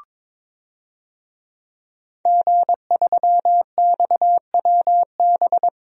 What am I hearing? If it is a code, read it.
Morse code, heard as a keyed tone, about 22 wpm, 700 Hz. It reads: G3XWB
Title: DXCC-100 Call Signs - 5 Characters - Repeated 3x - 22wpm